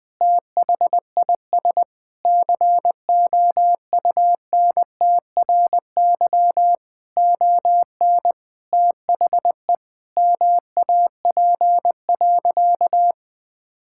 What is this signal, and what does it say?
Morse code, heard as a keyed tone, about 20 wpm, 700 Hz. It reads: THIS COUNTRY ON THE MAP.
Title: Sentences from Top 1000 Words - Review Part 6 - 20wpm